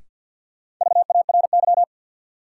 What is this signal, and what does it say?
Morse code, heard as a keyed tone, about 52 wpm, 700 Hz. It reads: VARY
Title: Sentences from Top 1200 Words - New Word Drill - 50wpm